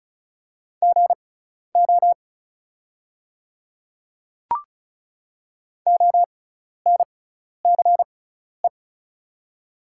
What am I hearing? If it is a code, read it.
Morse code, heard as a keyed tone, about 34 wpm, spaced as for 12 wpm, 700 Hz. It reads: GO ONCE
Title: 6x Character Spacing - Top 300 Words - 35wpm